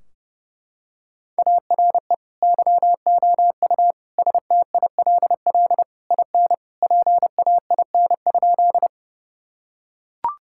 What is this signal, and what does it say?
Morse code, heard as a keyed tone, about 32 wpm, 700 Hz. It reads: ARE YOU STILL IN PAIN?
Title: Sentences from Top 1600 Words 30wpm